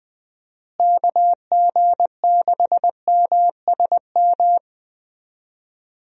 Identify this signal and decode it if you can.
Morse code, heard as a keyed tone, about 19 wpm, 700 Hz. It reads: KG6MSM
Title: USA Call Signs - 6 Characters - Repeated 3x - 20wpm